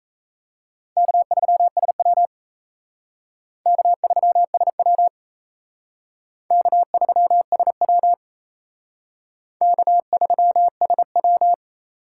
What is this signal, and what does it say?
Morse code, heard as a keyed tone, about 36 wpm, 700 Hz. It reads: K3SW K3SW K3SW K3SW